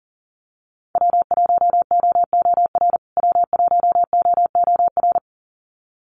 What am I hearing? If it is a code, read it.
Morse code, heard as a keyed tone, about 40 wpm, 700 Hz. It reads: W1OOR W1OOR
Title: USA Call Signs Repeated 2x 40wpm